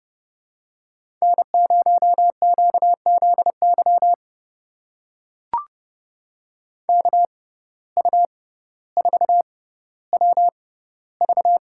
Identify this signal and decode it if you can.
Morse code, heard as a keyed tone, about 31 wpm, spaced as for 15 wpm, 700 Hz. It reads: N0QZY KU4WV
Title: USA Call Signs - 5 Characters - 6x Character Spacing - 30wpm